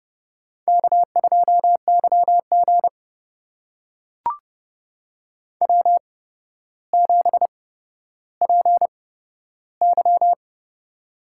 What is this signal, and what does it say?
Morse code, heard as a keyed tone, about 30 wpm, spaced as for 13 wpm, 700 Hz. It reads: K2YG W7PY
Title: USA Call Signs - 4 Characters - 8x Character Spacing - 30wpm